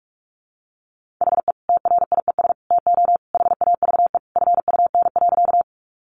Morse code, heard as a keyed tone, about 44 wpm, 700 Hz. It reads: HE TRIES TO HAVE FUN.
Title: Easy Phrases 45wpm